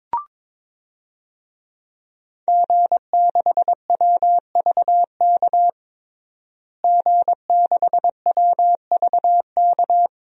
Morse code, heard as a keyed tone, about 21 wpm, 700 Hz. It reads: G6WVK G6WVK
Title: DXCC-100 Call Signs - 5 Characters - Repeated 3x - 22wpm